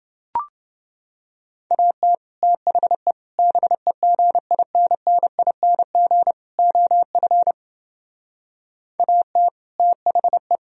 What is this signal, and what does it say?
Morse code, heard as a keyed tone, about 29 wpm, 700 Hz. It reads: AT THE BEGINNING OF AT THE